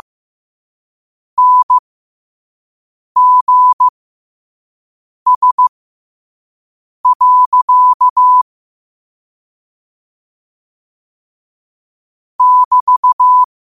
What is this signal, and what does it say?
Morse code, heard as a keyed tone, about 15 wpm, spaced as for 5 wpm, 1000 Hz. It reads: NGS. <BT>